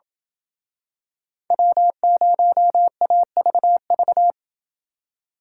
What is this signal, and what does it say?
Morse code, heard as a keyed tone, about 27 wpm, 700 Hz. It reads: W0AVV